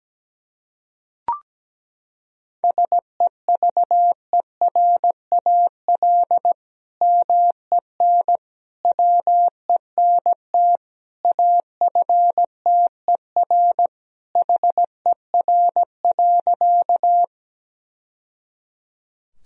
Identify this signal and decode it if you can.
Morse code, heard as a keyed tone, about 17 wpm, 700 Hz. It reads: SEVERAL MEN WENT AFTER HER.
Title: Sentences from Top 1000 Words - Review Part 2 - 17wpm